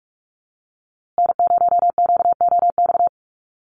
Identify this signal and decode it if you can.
Morse code, heard as a keyed tone, about 44 wpm, 690 Hz. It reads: N0QOX
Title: USA Call Signs Repeated 3x 45wpm